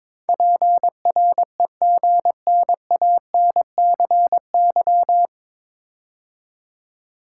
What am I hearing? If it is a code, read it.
Morse code, heard as a keyed tone, about 22 wpm, 700 Hz. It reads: PREGNANCY
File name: 9 Letter Words - First 3 Letters Given - 22wpm